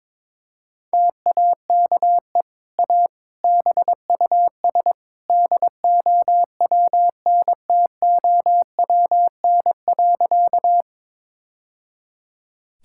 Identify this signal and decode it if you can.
Morse code, heard as a keyed tone, about 22 wpm, 700 Hz. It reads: TAKE A BUS DOWNTOWN.
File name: practice / 22wpm / Easy Sentences with Hints - 22wpm